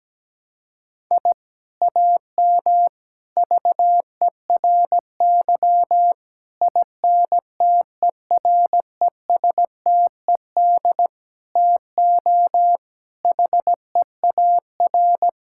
Morse code, heard as a keyed tone, about 17 wpm, 700 Hz. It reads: I AM VERY INTERESTED TO HEAR
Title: Sentences from Top 1000 Words - Review Part 6 - 17wpm